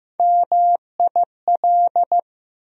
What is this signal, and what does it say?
Morse code, heard as a keyed tone, about 15 wpm, 700 Hz. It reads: MIL
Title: USA Call Sign Postfixes - 3 Letters 15wpm